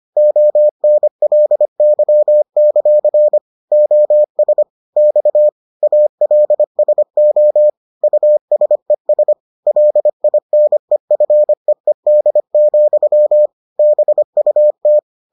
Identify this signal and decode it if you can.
Morse code, heard as a keyed tone, about 25 wpm, 600 Hz. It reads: ONLY; OS X ALSO USES LINEFEED, BUT